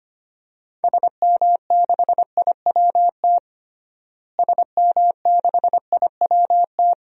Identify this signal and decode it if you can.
Morse code, heard as a keyed tone, about 25 wpm, 700 Hz. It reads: SM6IWT SM6IWT
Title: DXCC-100 Call Signs - 6 Characters - Repeated 2x - 25wpm